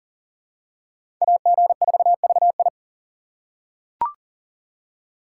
Morse code, heard as a keyed tone, about 40 wpm, 700 Hz. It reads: AG4VI
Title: USA Call Signs - 5 Characters - 4x Character Spacing - 40wpm